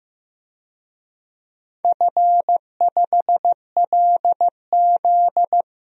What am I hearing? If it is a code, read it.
Morse code, heard as a keyed tone, about 16 wpm, 700 Hz. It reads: F5LZ
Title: DXCC-100 Call Signs - 4 Characters - Repeated 2x - 15wpm